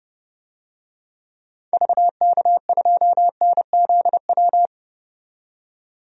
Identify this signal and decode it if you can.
Morse code, heard as a keyed tone, about 30 wpm, 700 Hz. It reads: VK2NZW
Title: DXCC-100 Call Signs - 6 Characters - 30wpm